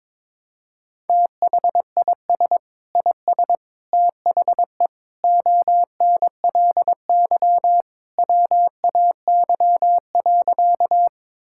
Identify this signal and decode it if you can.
Morse code, heard as a keyed tone, about 21 wpm, 700 Hz. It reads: THIS IS THE ONLY WAY.
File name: practice / 22wpm / Sentences with 2-Word Phrases - 22wpm